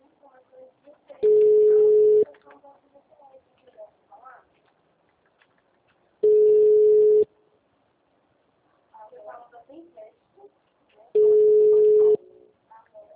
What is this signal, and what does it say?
Ringback tone (Europe, 425 Hz), with speech.